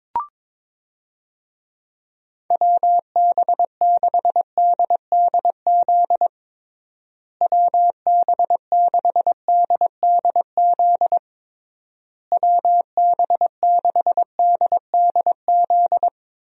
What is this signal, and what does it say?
Morse code, heard as a keyed tone, about 21 wpm, 700 Hz. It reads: WB6DDZ WB6DDZ WB6DDZ
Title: USA Call Signs - 6 Characters - Repeated 3x - 22wpm